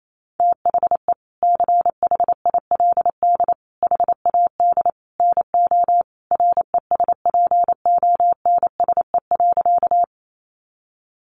Morse code, heard as a keyed tone, about 28 wpm, 700 Hz. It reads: THE CHILD HAD NO RESPONSE.
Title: Sentences from Top 1400 Words 28wpm